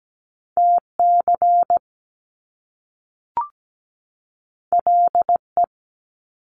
Morse code, heard as a keyed tone, about 17 wpm, 700 Hz. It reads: TC LE
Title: Top Two Letter Combinations in Words 17wpm